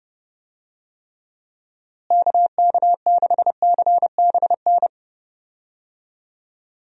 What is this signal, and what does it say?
Morse code, heard as a keyed tone, about 30 wpm, 700 Hz. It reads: KK6CBN
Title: USA Call Signs - 6 Characters - Repeated 2x - 30wpm